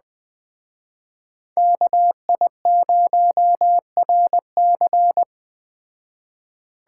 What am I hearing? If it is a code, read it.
Morse code, heard as a keyed tone, about 21 wpm, 700 Hz. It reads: KI0RC
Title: Speed Racing - USA 2x2 Call Signs 17wpm